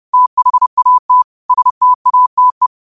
Morse code, heard as a keyed tone, about 30 wpm, 1000 Hz. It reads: THAT STATE